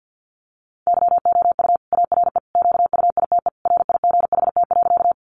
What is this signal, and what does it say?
Morse code, heard as a keyed tone, about 50 wpm, 700 Hz. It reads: YOU ARE QUITE RIGHT.
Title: Sentences from Top 700 Words 50wpm